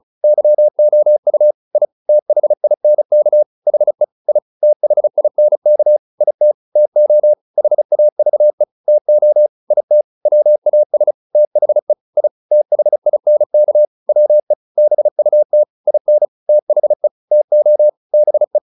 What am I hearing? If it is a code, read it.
Morse code, heard as a keyed tone, about 34 wpm, 600 Hz. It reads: YOU I THINK HE I THINK IT TO HAVE TO IT WAS THE I THINK WE BUT IN THE TO BE